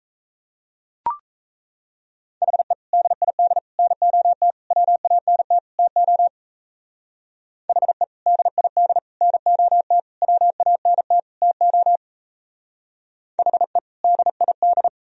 Morse code, heard as a keyed tone, about 32 wpm, 700 Hz. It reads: HE DID NOT WANT TO HE DID NOT WANT TO HE DID
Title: Speed-Racing - Sets of 5 Words 28wpm